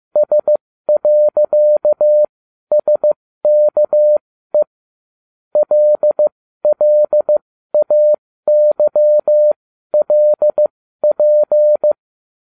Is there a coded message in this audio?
Yes: Morse code from 0.2 to 11.9 s, heard as a keyed tone, about 15 wpm, spaced as for 12 wpm, 600 Hz. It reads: S.SKE LLAYLP